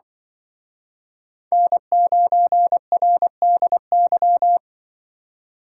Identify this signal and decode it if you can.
Morse code, heard as a keyed tone, about 24 wpm, 700 Hz. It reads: N9RDY
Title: Speed Racing - USA 1x3 Call Signs 20wpm